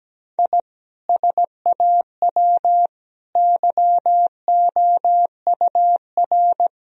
Morse code, heard as a keyed tone, about 17 wpm, 700 Hz. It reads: I SAW YOUR
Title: Sentences from Top 1000 Words - Review Part 1 - 17wpm